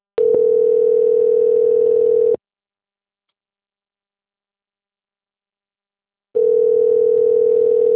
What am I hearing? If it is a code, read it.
Ringback tone (North America), 440 + 480 Hz.
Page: Cornelio Procopio 2